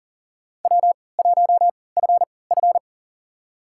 Morse code, heard as a keyed tone, about 40 wpm, spaced as for 23 wpm, 700 Hz. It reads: W1FF
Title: USA Call Signs - 4 Characters - 3x Character Spacing - 40wpm